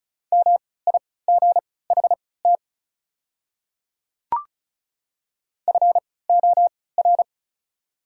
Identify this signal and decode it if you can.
Morse code, heard as a keyed tone, about 36 wpm, spaced as for 20 wpm, 690 Hz. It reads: MIGHT FOR
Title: 3x Character Spacing - Top 300 Words - 35wpm